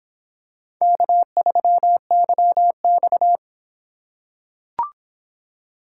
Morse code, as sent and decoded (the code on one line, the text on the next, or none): -.- ...-- -.-- -..-
K3YX